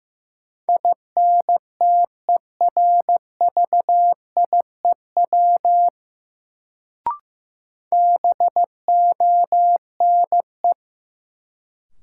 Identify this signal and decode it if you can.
Morse code, heard as a keyed tone, about 15 wpm, 700 Hz. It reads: INTERVIEW BONE
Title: Sentences from Top 1600 Words - New Word Drill Alt - 15wpm